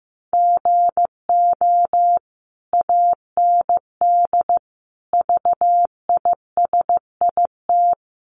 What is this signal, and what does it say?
Morse code, heard as a keyed tone, about 15 wpm, 700 Hz. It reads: GO AND VISIT
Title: Sentences from Top 1100 Words 15wpm